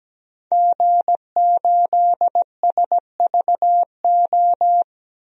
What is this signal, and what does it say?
Morse code, heard as a keyed tone, about 17 wpm, 700 Hz. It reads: G8SVO
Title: DXCC-100 Call Signs - 5 Characters - 17wpm